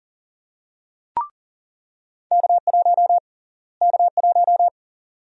Morse code, heard as a keyed tone, about 40 wpm, 700 Hz. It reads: K1 K1
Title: USA Call Sign Prefixes - Repeated 2x - 40wpm